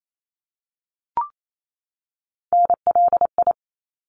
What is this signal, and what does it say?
Morse code, heard as a keyed tone, about 28 wpm, 700 Hz. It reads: NLI
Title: ARRL Sections 28wpm